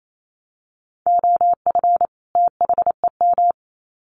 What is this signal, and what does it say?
Morse code, heard as a keyed tone, about 28 wpm, 700 Hz. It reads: OF THEM
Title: Sets of 2 Words 28wpm